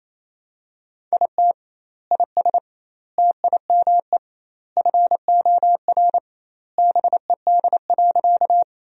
Morse code, heard as a keyed tone, about 28 wpm, 700 Hz. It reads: IT IS TIME FOR BED.
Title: 2x Word-Spacing - Sentences from Top 500 Words - 28wpm